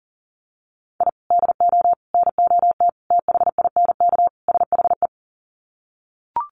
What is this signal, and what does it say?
Morse code, heard as a keyed tone, about 40 wpm, 700 Hz. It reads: I DO NOT THINK SHE
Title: Sets of 5 Words 40wpm